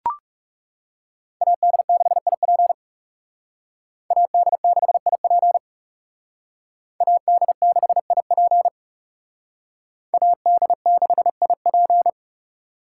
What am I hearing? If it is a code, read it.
Morse code, heard as a keyed tone, about 38 wpm, 700 Hz. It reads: AD6IP AD6IP AD6IP AD6IP